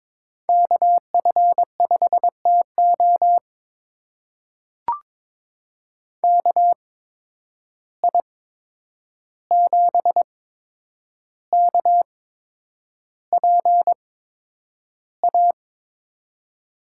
Morse code, heard as a keyed tone, about 22 wpm, spaced as for 9 wpm, 700 Hz. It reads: KF5TO KI7KPA
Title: USA Call Signs - 8x Character Spacing - 22wpm